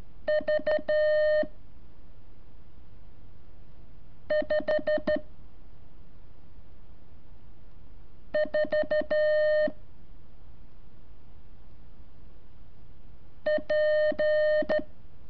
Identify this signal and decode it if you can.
Morse code, heard as a keyed tone, about 13 wpm, 610 Hz. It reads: V 5 4 P
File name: curso / ejercicio 23